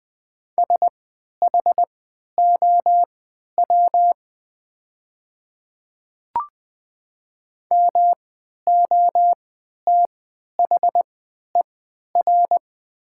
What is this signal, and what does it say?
Morse code, heard as a keyed tone, about 20 wpm, spaced as for 11 wpm, 700 Hz. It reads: SHOW MOTHER
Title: 3x Character Spacing - Top 300 Words - 20wpm